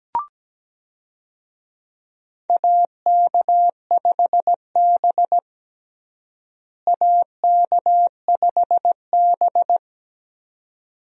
Morse code, heard as a keyed tone, about 17 wpm, 700 Hz. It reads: AK5B AK5B
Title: USA Call Signs - 4 Characters - Repeated 2x - 17wpm